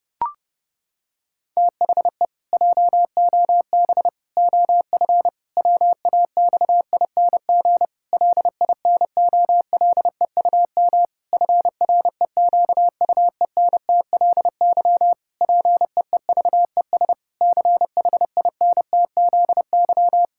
Morse code, heard as a keyed tone, about 32 wpm, 700 Hz. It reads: THE JOB OF WAXING LINOLEUM FREQUENTLY PEEVES CHINTZY
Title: Pangram Challenge v2 - 30wpm